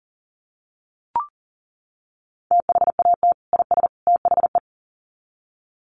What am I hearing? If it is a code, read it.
Morse code, heard as a keyed tone, about 40 wpm, 700 Hz. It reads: THAT IS THE
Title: Sets of 3 Words 40wpm